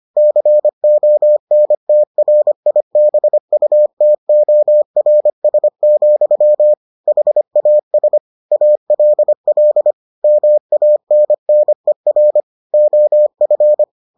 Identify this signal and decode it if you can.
Morse code, heard as a keyed tone, about 25 wpm, 600 Hz. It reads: CONTRIBUTORS, HAS ALL MANNER OF